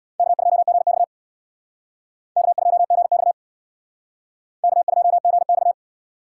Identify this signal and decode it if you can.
Morse code, heard as a keyed tone, about 65 wpm, 700 Hz. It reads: K2GX K2GX K2GX